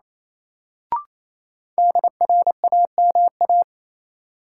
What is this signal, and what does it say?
Morse code, heard as a keyed tone, about 28 wpm, 700 Hz. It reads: DRAMA
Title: Sentences from Top 1500 Words - New Word Drill - 28wpm